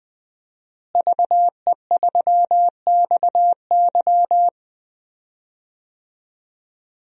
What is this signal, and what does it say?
Morse code, heard as a keyed tone, about 20 wpm, 700 Hz. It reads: VE3XY
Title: DXCC-100 Call Signs - 5 Characters - Repeated 3x - 20wpm